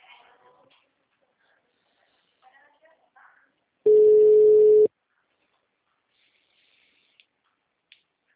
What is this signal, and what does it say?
Ringback tone (Europe, 425 Hz), with speech.